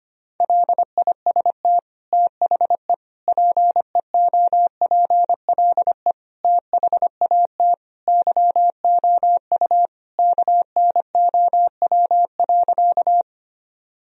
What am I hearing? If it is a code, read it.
Morse code, heard as a keyed tone, about 25 wpm, 700 Hz. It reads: LIST THE PEOPLE THAT YOU KNOW.
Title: Sentences from Top 1000 Words - Review Part 2 - 25wpm